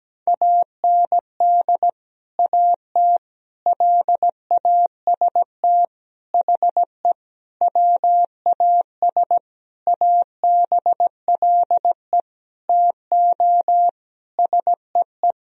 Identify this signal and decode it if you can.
Morse code, heard as a keyed tone, about 17 wpm, 700 Hz. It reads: AND AT LAST HE WAS ABLE TO SEE
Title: Sentences from Top 1000 Words - Review Part 2 - 17wpm